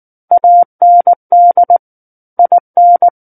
Morse code, heard as a keyed tone, about 19 wpm, 700 Hz. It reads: AND IN
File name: Book One / 10-Chapter 10 - 19 WPM